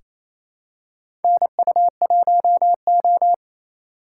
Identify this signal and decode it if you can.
Morse code, heard as a keyed tone, about 28 wpm, 700 Hz. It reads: NU1O
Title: K1USN SST QSO Practice - 28wpm